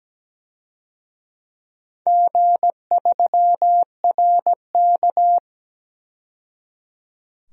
Morse code, heard as a keyed tone, about 17 wpm, 700 Hz. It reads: G3RK
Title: DXCC-100 Call Signs - 4 Characters - 17wpm